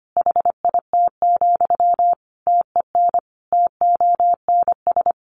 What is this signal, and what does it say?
Morse code, heard as a keyed tone, about 25 wpm, 700 Hz. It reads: HIT, TEN TONS
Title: A Sound Of Thunder 25wpm